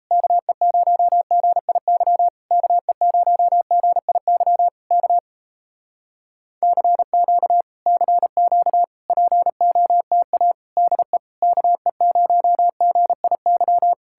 Morse code, heard as a keyed tone, about 38 wpm, 700 Hz. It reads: KE0GIY KE0GIY K CQ CQ POTA DE KE0GIY